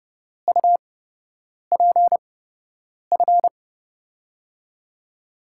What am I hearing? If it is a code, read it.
Morse code, heard as a keyed tone, about 30 wpm, spaced as for 8 wpm, 700 Hz. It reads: UPF
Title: USA Call Signs - 6 Characters - 8x Character Spacing - 30wpm